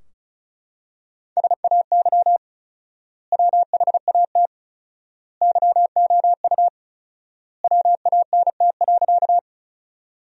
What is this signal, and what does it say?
Morse code, heard as a keyed tone, about 36 wpm, 700 Hz. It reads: SAY WHAT YOU WANT.
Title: 4x Word-Spacing - Sentences from Top 200 Words - 35wpm